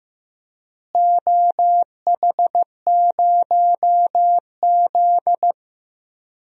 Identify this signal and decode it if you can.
Morse code, heard as a keyed tone, about 15 wpm, 700 Hz. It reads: OH0Z
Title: DXCC-100 Call Signs - 4 Characters - Repeated 2x - 15wpm